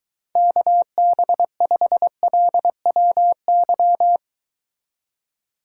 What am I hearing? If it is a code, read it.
Morse code, heard as a keyed tone, about 22 wpm, 700 Hz. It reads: KB5LWY